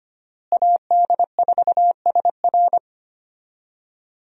Morse code, heard as a keyed tone, about 25 wpm, 700 Hz. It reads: AD4SR
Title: USA Call Signs - 5 Characters - Repeated 2x - 25wpm